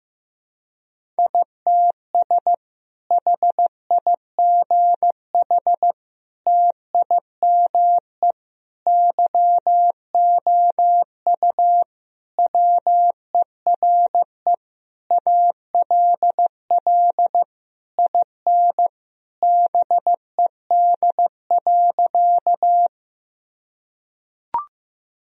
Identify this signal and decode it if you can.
Morse code, heard as a keyed tone, about 15 wpm, 700 Hz. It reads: ITS HIGH TIME YOU WERE ALL IN BED.
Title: Sentences from Top 1000 Words - Review Part 1 - 15wpm